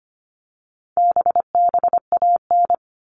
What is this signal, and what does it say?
Morse code, heard as a keyed tone, about 25 wpm, 700 Hz. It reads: BBAN
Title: All Cut Numbers - 4-Digit 25wpm